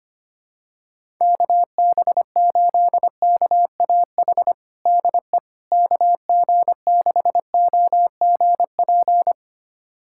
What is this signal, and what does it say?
Morse code, heard as a keyed tone, about 25 wpm, 700 Hz. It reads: KB8KAH DE KG6OGP
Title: USA Callsign DE Callsign - 6 Characters - 25wpm